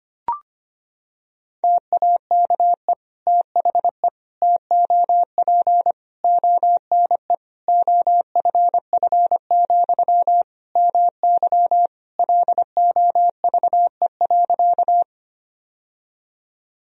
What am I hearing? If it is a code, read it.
Morse code, heard as a keyed tone, about 25 wpm, 700 Hz. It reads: TAKE THE TOP ONE OFF, MY LOVE.
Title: Sentences from Top 1000 Words - Review Part 2 - 25wpm